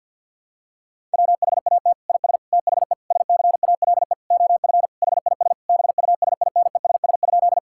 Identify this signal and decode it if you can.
Morse code, heard as a keyed tone, about 50 wpm, 700 Hz. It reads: WHAT IS THE SCALE OF HIS BUSINESS?
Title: Sentences from Top 1000 Words - Review Part 5 - 50wpm